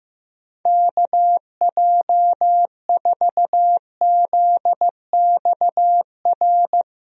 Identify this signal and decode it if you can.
Morse code, heard as a keyed tone, about 15 wpm, 700 Hz. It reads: KJ4ZXR
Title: USA Call Signs - 6 Characters - 15wpm